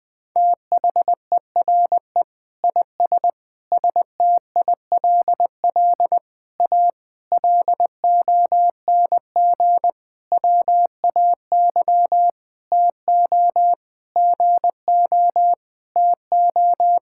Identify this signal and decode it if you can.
Morse code, heard as a keyed tone, about 21 wpm, 700 Hz. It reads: THERE IS STILL A LONG WAY TO GO TO
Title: Sentences with 8-Word Phrases - 20wpm